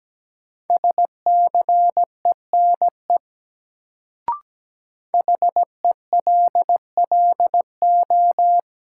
Morse code, heard as a keyed tone, about 17 wpm, 700 Hz. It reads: SCENE HELLO
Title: Sentences from Top 1100 Words - New Word Drill - 17wpm